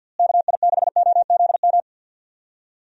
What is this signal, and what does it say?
Morse code, heard as a keyed tone, about 49 wpm, 700 Hz. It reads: KI6OZM
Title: USA Call Signs - 6 Characters - Repeated 3x - 50wpm